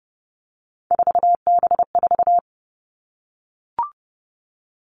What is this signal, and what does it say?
Morse code, heard as a keyed tone, about 30 wpm, 700 Hz. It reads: 4B4
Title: Three Challenging Letter-Number Combinations 30wpm